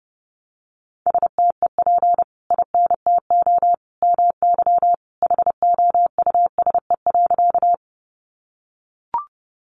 Morse code, heard as a keyed tone, about 28 wpm, 700 Hz. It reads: STEP INTO MY HOUSE.
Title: Sentences from Top 600 Words 30wpm